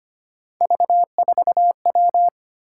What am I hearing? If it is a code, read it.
Morse code, heard as a keyed tone, about 25 wpm, 700 Hz. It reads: V4W
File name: Three Letter-Number Combinations - 25wpm